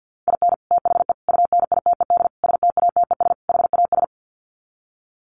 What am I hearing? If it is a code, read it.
Morse code, heard as a keyed tone, about 50 wpm, 700 Hz. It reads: IN THE UNITED STATES HAS
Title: Sets of 5 Words Encore2 50wpm